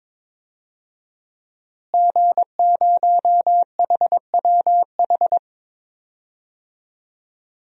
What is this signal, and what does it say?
Morse code, heard as a keyed tone, about 22 wpm, 700 Hz. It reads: G0HWH
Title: DXCC-100 Call Signs - 5 Characters - Repeated 3x - 22wpm